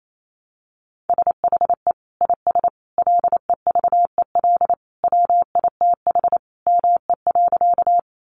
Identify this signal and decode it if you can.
Morse code, heard as a keyed tone, about 28 wpm, 700 Hz. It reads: SHE IS LEVEL WITH ME.
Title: Sentences from Top 700 Words 28wpm